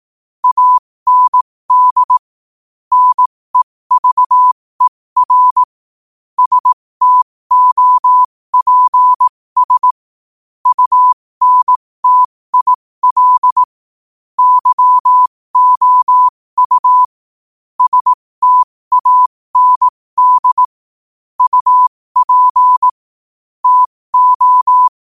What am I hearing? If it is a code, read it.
Morse code, heard as a keyed tone, about 18 wpm, spaced as for 15 wpm, 1000 Hz. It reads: AND NEVER STOPS UNTIL YOU STAND UP TO
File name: qotdfiles / 15wpm